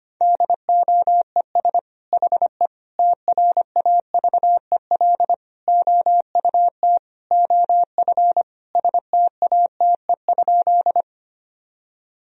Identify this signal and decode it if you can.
Morse code, heard as a keyed tone, about 25 wpm, 700 Hz. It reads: DOES HE TRAVEL OUT OF STATE?
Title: Sentences from Top 1000 Words - Review Part 2 - 25wpm